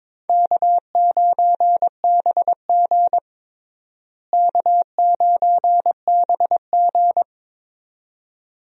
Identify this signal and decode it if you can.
Morse code, heard as a keyed tone, about 22 wpm, 700 Hz. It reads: K9BG K9BG
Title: USA Call Signs - 4 Characters - Repeated 2x - 22wpm